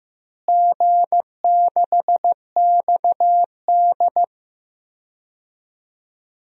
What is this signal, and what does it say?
Morse code, heard as a keyed tone, about 15 wpm, 700 Hz. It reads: G6XD
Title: DXCC-100 Call Signs - 4 Characters - Repeated 3x - 15wpm